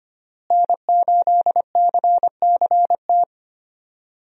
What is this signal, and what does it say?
Morse code, heard as a keyed tone, about 25 wpm, 700 Hz. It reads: N8CCT
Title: USA Call Signs - 5 Characters - Repeated 3x - 25wpm